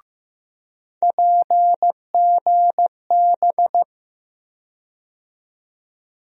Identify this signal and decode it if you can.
Morse code, heard as a keyed tone, about 15 wpm, 700 Hz. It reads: PGB
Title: USA Call Sign Postfixes - 3 Letters 15wpm